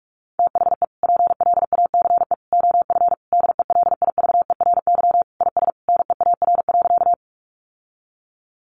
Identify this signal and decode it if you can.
Morse code, heard as a keyed tone, about 44 wpm, 700 Hz. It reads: THE PLACE OF DELIVERY IS NEAR.
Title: Sentences from Top 1500 Words 45wpm